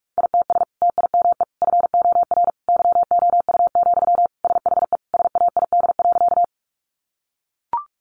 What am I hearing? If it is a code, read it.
Morse code, heard as a keyed tone, about 44 wpm, 700 Hz. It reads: ITS TIME FOR YOU, SHE SAID.
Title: Sentences from Top 600 Words 45wpm